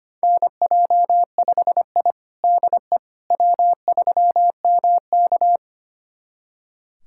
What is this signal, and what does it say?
Morse code, heard as a keyed tone, about 25 wpm, 700 Hz. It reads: NJ5I DE W3MK
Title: USA Callsign DE Callsign - 4 Characters - 25wpm